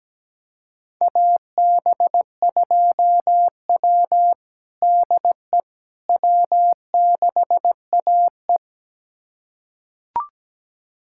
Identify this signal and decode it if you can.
Morse code, heard as a keyed tone, about 17 wpm, 700 Hz. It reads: AB2W DE W6AE
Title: USA Callsign DE Callsign - 4 Characters - 17wpm